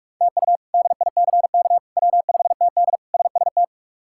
Morse code, heard as a keyed tone, about 42 wpm, 700 Hz. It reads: TU DICK W5TD SST
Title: K1USN SST QSO Practice - 45wpm